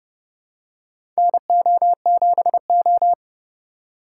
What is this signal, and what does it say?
Morse code, heard as a keyed tone, about 30 wpm, 700 Hz. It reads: NO7O
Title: USA Call Signs - 4 Characters - 4x Character Spacing - 30wpm